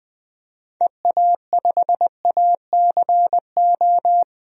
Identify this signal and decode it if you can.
Morse code, heard as a keyed tone, about 21 wpm, 700 Hz. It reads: EA5ACO
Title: DXCC-100 Call Signs - 6 Characters - 20wpm